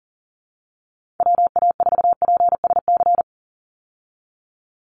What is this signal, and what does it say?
Morse code, heard as a keyed tone, about 40 wpm, 700 Hz. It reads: WA4PSC
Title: Call Signs - 40wpm